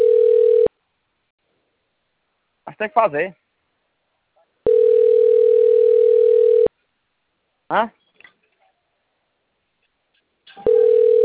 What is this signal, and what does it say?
Ringback tone (North America), 440 + 480 Hz, with speech.
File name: rg-1600-unknown- - - .35287